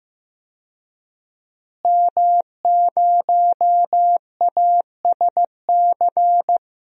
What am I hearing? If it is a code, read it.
Morse code, heard as a keyed tone, about 15 wpm, 700 Hz. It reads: M0ASC
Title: DXCC-100 Call Signs - 5 Characters - Repeated 2x - 15wpm